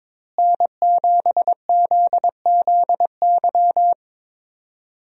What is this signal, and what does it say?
Morse code, heard as a keyed tone, about 22 wpm, 700 Hz. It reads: N7ZZY
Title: USA Call Signs - 5 Characters - Repeated 3x - 22wpm